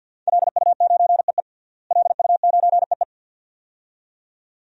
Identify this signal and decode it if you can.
Morse code, heard as a keyed tone, about 47 wpm, 700 Hz. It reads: LU9EE LU9EE
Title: DXCC-100 Call Signs - 5 Characters - Repeated 2x - 50wpm